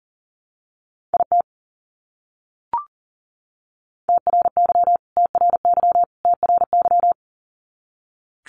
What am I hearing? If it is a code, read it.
Morse code, heard as a keyed tone, about 39 wpm, 700 Hz. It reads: IT TRY TRY TRY
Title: Top 200 Words Repeated 40wpm